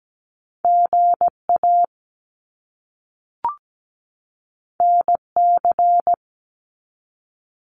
Morse code, heard as a keyed tone, about 17 wpm, 700 Hz. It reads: GA NC
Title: US States 17wpm